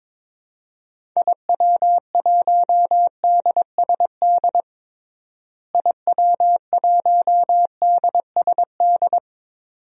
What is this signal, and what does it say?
Morse code, heard as a keyed tone, about 22 wpm, 700 Hz. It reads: IW1DSD IW1DSD
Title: DXCC-100 Call Signs - 6 Characters - Repeated 2x - 22wpm